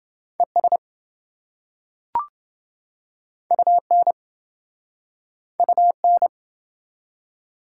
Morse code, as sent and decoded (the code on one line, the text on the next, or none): . ... / ..- -. / ..- -.
ES UN UN